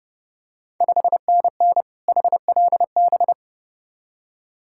Morse code, heard as a keyed tone, about 30 wpm, 690 Hz. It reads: 5NN HLB
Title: Ontario QSO Party Exchange 30wpm